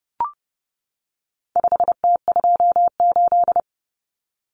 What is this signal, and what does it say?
Morse code, heard as a keyed tone, about 30 wpm, 700 Hz. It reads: 5T28
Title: Common Cut Numbers - 4-Digit 30wpm